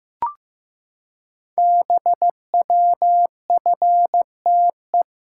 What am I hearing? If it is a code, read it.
Morse code, heard as a keyed tone, about 15 wpm, 700 Hz. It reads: BWFTE
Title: Code Groups - 5 Letters - 15wpm